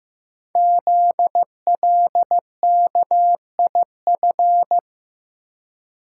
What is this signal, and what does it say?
Morse code, heard as a keyed tone, about 15 wpm, 700 Hz. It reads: ZLKIF
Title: Code Groups - 5 Letters - 15wpm